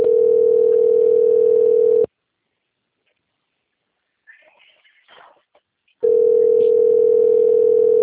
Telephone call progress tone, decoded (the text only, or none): ringback tone